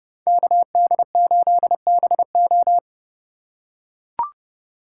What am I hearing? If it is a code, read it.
Morse code, heard as a keyed tone, about 32 wpm, 700 Hz. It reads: KD8BO
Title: USA Call Signs - 5 Characters - 4x Character Spacing - 30wpm